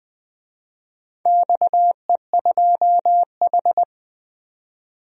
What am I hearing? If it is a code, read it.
Morse code, heard as a keyed tone, about 20 wpm, 700 Hz. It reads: XE2H
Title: DXCC-100 Call Signs - 4 Characters - 20wpm